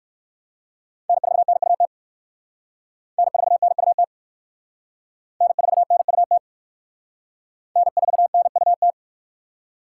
Morse code, heard as a keyed tone, about 60 wpm, 700 Hz. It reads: N4NUT N4NUT N4NUT N4NUT